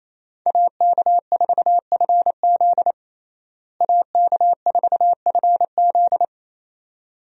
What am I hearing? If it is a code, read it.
Morse code, heard as a keyed tone, about 28 wpm, 700 Hz. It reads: AK4FZ AK4FZ
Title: USA Call Signs - 5 Characters - Repeated 2x - 28wpm